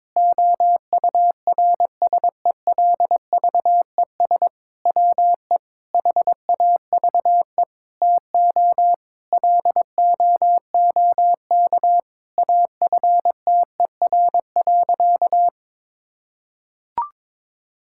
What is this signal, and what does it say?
Morse code, heard as a keyed tone, about 22 wpm, 700 Hz. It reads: OURSELVES WE HAVE TO LOOK AFTER.
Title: Sentences from Top 1600 Words 22wpm